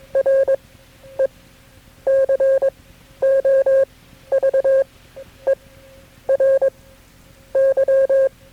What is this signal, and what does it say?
Morse code, heard as a keyed tone, about 22 wpm, spaced as for 10 wpm, 540 Hz. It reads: RECOVERY
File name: KB6MT-tape3-side1